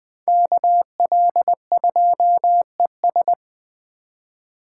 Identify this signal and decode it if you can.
Morse code, heard as a keyed tone, about 21 wpm, 700 Hz. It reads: KL2ES